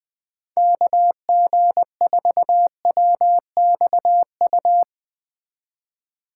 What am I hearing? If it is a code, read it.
Morse code, heard as a keyed tone, about 20 wpm, 700 Hz. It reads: KG4WXU
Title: 2023 POTA Hunter Call Signs - 20wpm